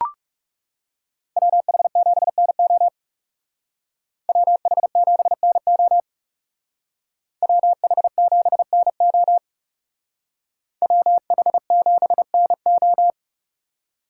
Morse code, heard as a keyed tone, about 37 wpm, 700 Hz. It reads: WH7NO WH7NO WH7NO WH7NO
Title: Speed Racing - USA 2x2 Call Signs 30wpm